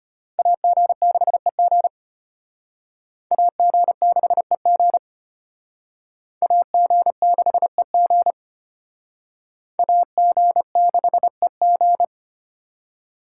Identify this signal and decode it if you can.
Morse code, heard as a keyed tone, about 32 wpm, 700 Hz. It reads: AG6EG AG6EG AG6EG AG6EG